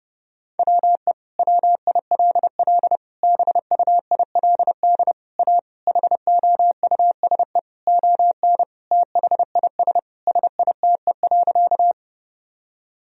Morse code, heard as a keyed tone, about 30 wpm, 700 Hz. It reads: WE WILL BUILD A HOUSE ON THIS SITE.
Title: Sentences from Top 1000 Words - Review Part 4 - 30wpm